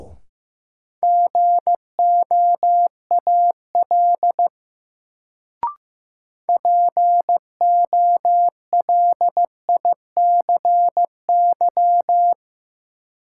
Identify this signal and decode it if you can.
Morse code, heard as a keyed tone, about 15 wpm, 700 Hz. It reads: GOAL POLICY
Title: Sentences from Top 1000 Words - Review Part 5 - New Word Drill - 15wpm